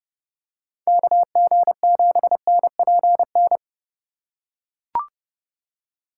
Morse code, heard as a keyed tone, about 29 wpm, 700 Hz. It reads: KG7NPN